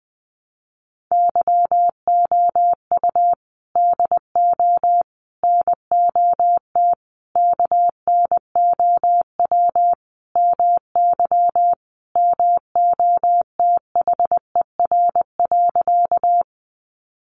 Morse code, heard as a keyed tone, about 20 wpm, 700 Hz. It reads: YOU DO NOT KNOW MY MOTHER.
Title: Sentences from Top 600 Words 20wpm